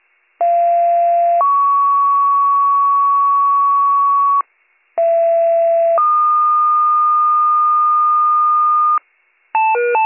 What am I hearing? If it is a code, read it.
Two-tone page, heard as a one-second tone, then a three-second tone, about 690 then 1120 Hz, about 670 then 1220 Hz. It is heard from 0.4 to 9.0 s.